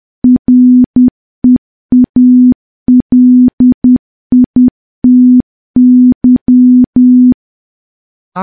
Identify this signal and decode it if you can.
Morse code, heard as a keyed tone, about 10 wpm, 250 Hz. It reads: REALITY
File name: morse / morse-training-10WPM